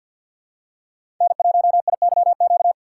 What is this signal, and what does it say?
Morse code, heard as a keyed tone, about 46 wpm, 700 Hz. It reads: N1IYQ
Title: USA Call Signs - 5 Characters - 4x Character Spacing - 50wpm